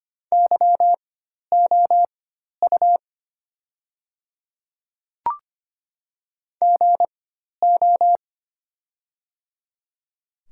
Morse code, heard as a keyed tone, about 25 wpm, spaced as for 12 wpm, 700 Hz. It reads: YOU GO
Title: 4x Character Spacing - Top 200 Words - 25wpm